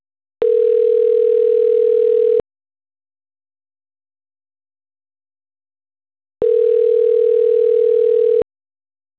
Ringback tone (North America), 440 + 480 Hz.